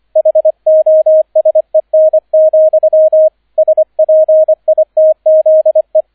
Morse code, heard as a keyed tone, about 25 wpm, 630 Hz, with radio noise. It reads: HOSEN, SPITZE